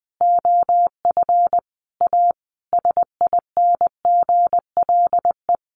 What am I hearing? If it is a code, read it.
Morse code, heard as a keyed tone, about 20 wpm, 700 Hz. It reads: OF A SINGLE
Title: Sentences from Top 1400 Words 20wpm